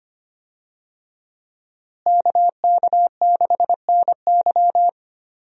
Morse code, heard as a keyed tone, about 25 wpm, 700 Hz. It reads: KK6NY
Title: USA Call Signs - 5 Characters - Repeated 2x - 25wpm